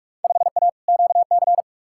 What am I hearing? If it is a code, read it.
Morse code, heard as a keyed tone, about 43 wpm, 700 Hz. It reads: 5A QC
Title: Field Day Exchanges Encore - 45wpm